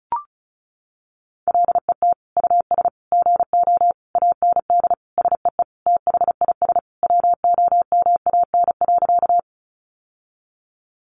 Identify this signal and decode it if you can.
Morse code, heard as a keyed tone, about 34 wpm, 700 Hz. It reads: LET US GO AND SEE THIS WOMAN.
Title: Sentences from Top 1100 Words 35wpm